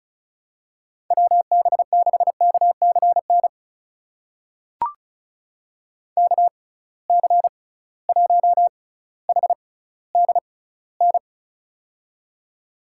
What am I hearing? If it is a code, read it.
Morse code, heard as a keyed tone, about 34 wpm, spaced as for 18 wpm, 700 Hz. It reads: WB6KCN KC1HDN
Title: USA Call Signs - 6 Characters - 6x Character Spacing - 35wpm